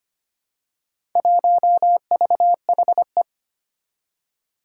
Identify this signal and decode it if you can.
Morse code, heard as a keyed tone, about 25 wpm, 700 Hz. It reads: 1VHE
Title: Four Challenging Letter-Number Combinations 25wpm